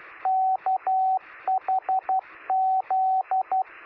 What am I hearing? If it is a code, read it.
Morse code, heard as a keyed tone, about 12 wpm, 760 Hz, with radio noise. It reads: KHZ
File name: sm7lcb / 1959z oh3shf 9cm